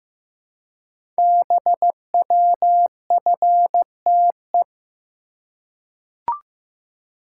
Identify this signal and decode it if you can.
Morse code, heard as a keyed tone, about 15 wpm, 700 Hz. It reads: BWFTE
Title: Code Groups - 5 Letters - 15wpm